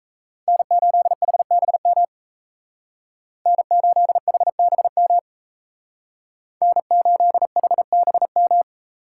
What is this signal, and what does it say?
Morse code, heard as a keyed tone, about 34 wpm, 700 Hz. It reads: N8HBM N8HBM N8HBM
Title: Speed Racing - USA 1x3 Call Signs 28wpm